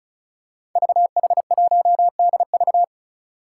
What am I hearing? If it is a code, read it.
Morse code, heard as a keyed tone, about 36 wpm, 700 Hz. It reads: VH1DV
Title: Five Challenging Letter-Number Combinations 35wpm